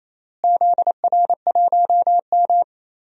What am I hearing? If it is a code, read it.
Morse code, heard as a keyed tone, about 27 wpm, 700 Hz. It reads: ZR1M
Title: DXCC-100 Call Signs - 4 Characters - Repeated 3x - 28wpm